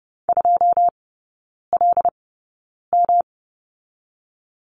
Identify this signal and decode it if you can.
Morse code, heard as a keyed tone, about 32 wpm, spaced as for 9 wpm, 700 Hz. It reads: 2LM
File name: Lesson 18e - Letter m - Three Characters 30wpm